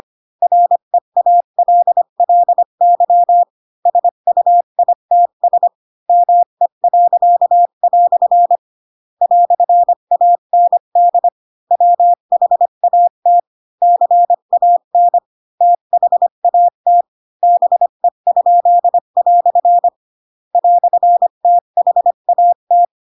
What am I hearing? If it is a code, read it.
Morse code, heard as a keyed tone, about 25 wpm, 700 Hz. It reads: REALLY SUITS ME." "AND WHAT CAN THAT BE?" "THAT
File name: Pinocchio / Pinocchio ch4 20 25 ik1hge